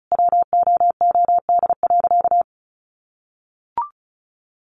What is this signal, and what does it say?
Morse code, heard as a keyed tone, about 34 wpm, 700 Hz. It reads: WOOD.
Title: Sentences from Top 1200 Words 35wpm